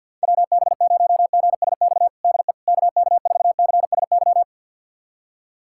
Morse code, heard as a keyed tone, about 50 wpm, 700 Hz. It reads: WB0GSX DE KK4CSY
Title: USA Callsign DE Callsign - 6 Characters - 50wpm